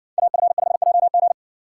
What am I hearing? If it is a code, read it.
Morse code, heard as a keyed tone, about 56 wpm, 700 Hz. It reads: AF5JG